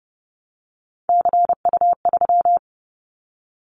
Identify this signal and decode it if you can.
Morse code, heard as a keyed tone, about 30 wpm, 700 Hz. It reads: CU3
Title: Callsign Prefixes for Top 100 DXCC - 30wpm